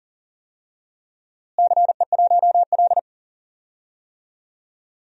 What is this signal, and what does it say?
Morse code, heard as a keyed tone, about 37 wpm, 700 Hz. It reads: CE1L
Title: DXCC-100 Call Signs - 4 Characters - Repeated 3x - 40wpm